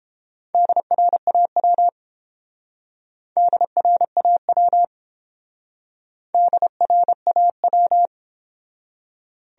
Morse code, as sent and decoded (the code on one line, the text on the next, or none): -.. .-. .- .-- / -.. .-. .- .-- / -.. .-. .- .--
DRAW DRAW DRAW